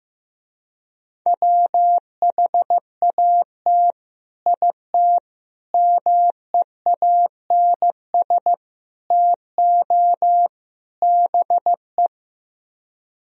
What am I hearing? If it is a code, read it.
Morse code, heard as a keyed tone, about 15 wpm, 700 Hz. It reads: WHAT IT MEANS TO BE
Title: Speed-Racing - Sets of 5 Words 15wpm